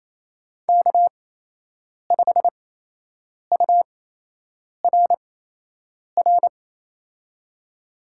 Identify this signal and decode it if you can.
Morse code, heard as a keyed tone, about 28 wpm, spaced as for 8 wpm, 700 Hz. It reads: K5URR
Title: USA Call Signs - 8x Character Spacing - 28wpm